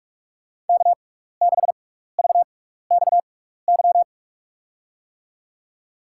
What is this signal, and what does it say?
Morse code, heard as a keyed tone, about 44 wpm, spaced as for 15 wpm, 700 Hz. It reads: K6VXY
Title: USA Call Signs - 5 Characters - 6x Character Spacing - 45wpm